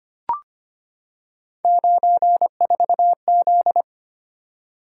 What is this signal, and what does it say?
Morse code, heard as a keyed tone, about 25 wpm, 700 Hz. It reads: 94Z
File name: Three Letter-Number Combinations - 25wpm